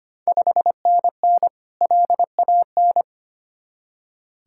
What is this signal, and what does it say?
Morse code, heard as a keyed tone, about 25 wpm, 700 Hz. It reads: 5NN LAN
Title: Ontario QSO Party Exchange 25wpm